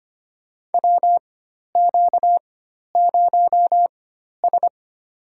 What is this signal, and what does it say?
Morse code, heard as a keyed tone, about 25 wpm, spaced as for 12 wpm, 700 Hz. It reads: WQ0S
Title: USA Call Signs - 4 Characters - 4x Character Spacing - 25wpm